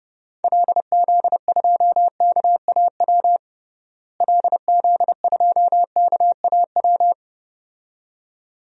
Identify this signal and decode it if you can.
Morse code, heard as a keyed tone, about 30 wpm, 700 Hz. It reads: LZ2KAW LZ2KAW
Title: DXCC-100 Call Signs - 6 Characters - Repeated 2x - 30wpm